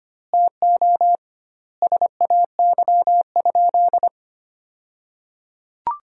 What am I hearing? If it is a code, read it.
Morse code, heard as a keyed tone, about 25 wpm, 700 Hz. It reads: TO SAY?
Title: 2x Word-Spacing - Sentences from Top 200 Words - 25wpm